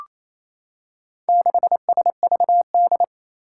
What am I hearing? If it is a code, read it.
Morse code, heard as a keyed tone, about 28 wpm, 700 Hz. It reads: 6SVD
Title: Four Challenging Letter-Number Combinations 28wpm